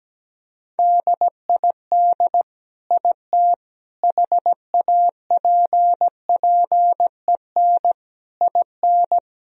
Morse code, heard as a keyed tone, about 17 wpm, 700 Hz. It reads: DID IT HAPPEN IN
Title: Sentences from Top 1000 Words - Review Part 2 - 17wpm